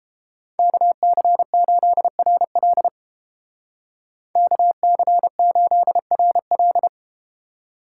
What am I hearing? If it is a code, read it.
Morse code, heard as a keyed tone, about 31 wpm, 700 Hz. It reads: KC8RL KC8RL